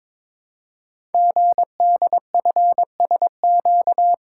Morse code, heard as a keyed tone, about 22 wpm, 700 Hz. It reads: GDFSQ
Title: Code Groups - 5 Letters - 22wpm